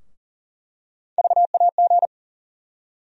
Morse code, heard as a keyed tone, about 40 wpm, 700 Hz. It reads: VAG
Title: USA Call Sign Postfixes - 3 Letters 40wpm